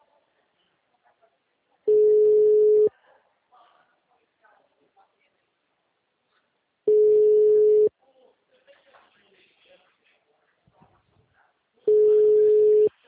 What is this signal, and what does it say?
Ringback tone (Europe, 425 Hz), with speech.
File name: out- -2006- - - .96367